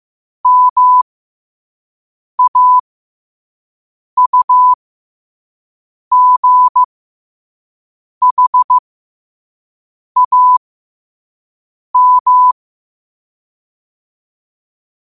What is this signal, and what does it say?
Morse code, heard as a keyed tone, about 15 wpm, spaced as for 5 wpm, 1000 Hz. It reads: MAUGHAM
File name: qotdfiles / 5wpm